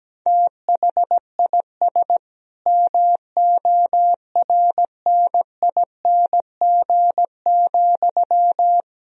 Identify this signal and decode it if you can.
Morse code, heard as a keyed tone, about 17 wpm, 700 Hz. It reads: THIS MORNING,
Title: Sentences from Top 1000 Words - Review Part 1 - 17wpm